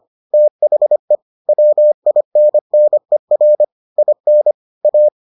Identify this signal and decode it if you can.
Morse code, heard as a keyed tone, about 25 wpm, 600 Hz. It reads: THE WINNER IN A